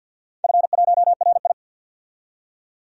Morse code, heard as a keyed tone, about 49 wpm, 700 Hz. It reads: F1RI
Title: DXCC-100 Call Signs - 4 Characters - 50wpm